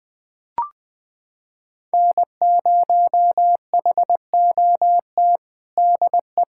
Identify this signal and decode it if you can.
Morse code, heard as a keyed tone, about 20 wpm, 700 Hz. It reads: N0HOT DE
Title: USA Callsign DE Callsign - 5 Characters - 20wpm